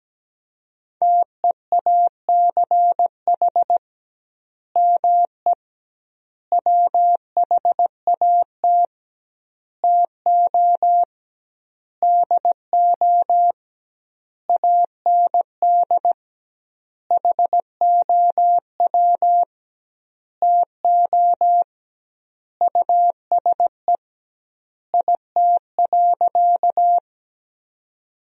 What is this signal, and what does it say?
Morse code, heard as a keyed tone, about 17 wpm, 700 Hz. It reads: TEACH ME WHAT TO DO AND HOW TO USE IT.
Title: 2x Word-Spacing - Sentences from Top 500 Words - 17wpm